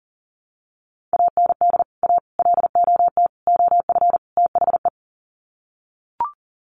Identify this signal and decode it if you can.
Morse code, heard as a keyed tone, about 40 wpm, 700 Hz. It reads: AND A LOT OF THE
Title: Sets of 5 Words Encore 40wpm